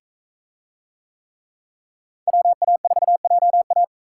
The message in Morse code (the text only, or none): WA4JA